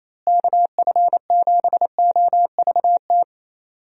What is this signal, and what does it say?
Morse code, heard as a keyed tone, about 28 wpm, 700 Hz. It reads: KF7OVT
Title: USA Call Signs - 6 Characters - Repeated 2x - 28wpm